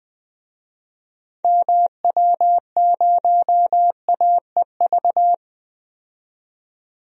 Morse code, heard as a keyed tone, about 20 wpm, 700 Hz. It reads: MW0AEV
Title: DXCC-100 Call Signs - 6 Characters - 20wpm